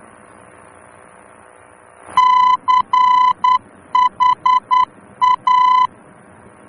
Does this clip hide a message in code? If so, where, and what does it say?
Yes: Morse code from 2.2 to 5.9 s, heard as a keyed tone, about 10 wpm, 1010 Hz. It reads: CHA